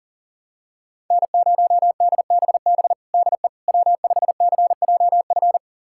Morse code, heard as a keyed tone, about 40 wpm, 700 Hz. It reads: N0DBB DE W5CJF
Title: USA Callsign DE Callsign - 5 Characters - 40wpm